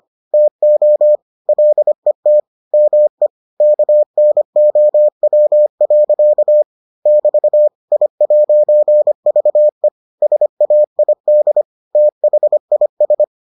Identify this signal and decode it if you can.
Morse code, heard as a keyed tone, about 25 wpm, 600 Hz. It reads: TO LET ME KNOW. <BT> I'VE SAID THIS